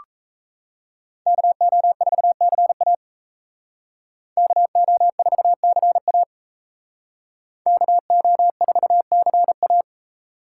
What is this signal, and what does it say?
Morse code, heard as a keyed tone, about 37 wpm, 700 Hz. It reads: KO4CA KO4CA KO4CA